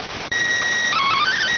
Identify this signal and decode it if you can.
SSTV (slow-scan television), PD 290.